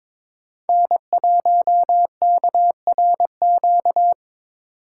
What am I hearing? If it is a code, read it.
Morse code, heard as a keyed tone, about 22 wpm, 700 Hz. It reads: N1KRQ
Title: USA Call Signs - 5 Characters - 4x Character Spacing - 22wpm